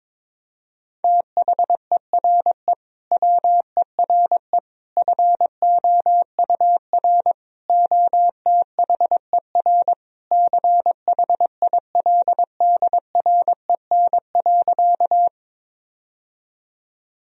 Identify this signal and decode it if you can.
Morse code, heard as a keyed tone, about 21 wpm, 700 Hz. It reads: THERE WERE FOUR OTHER CHILDREN.
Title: Sentences from Top 1000 Words - Review Part 1 - 22wpm